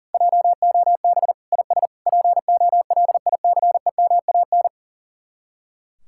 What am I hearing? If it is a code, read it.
Morse code, heard as a keyed tone, about 35 wpm, 700 Hz. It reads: JOB IS POLICEMAN
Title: Speed-Racing Common QSO Phrases - 40wpm